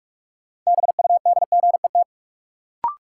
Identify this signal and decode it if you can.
Morse code, heard as a keyed tone, about 46 wpm, 700 Hz. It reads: BUDGET
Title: Sentences from Top 1200 Words - New Word Drill - 45wpm